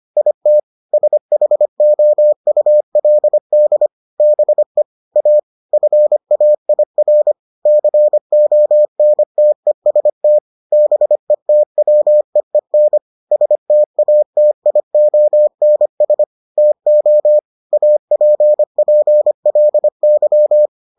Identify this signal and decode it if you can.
Morse code, heard as a keyed tone, about 25 wpm, 600 Hz. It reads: IT SHOULD BE A FAIR CONTEST BETWEEN STATIONS TO APPLY